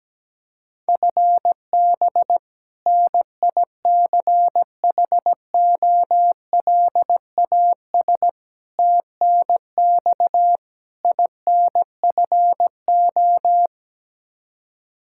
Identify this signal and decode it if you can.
Morse code, heard as a keyed tone, about 17 wpm, 700 Hz. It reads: FB NICHOLAS TNX INFO
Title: K1USN SST QSO Practice 17wpm